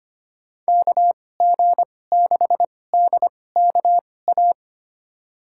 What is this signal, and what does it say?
Morse code, heard as a keyed tone, about 25 wpm, spaced as for 18 wpm, 700 Hz. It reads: KG6DKA
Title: USA Call Signs - 6 Characters - 2x Character Spacing - 25wpm